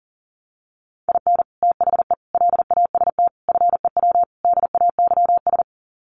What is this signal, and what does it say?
Morse code, heard as a keyed tone, about 40 wpm, 700 Hz. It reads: IN THE LAST FEW DAYS
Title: Sets of 5 Words Encore2 40wpm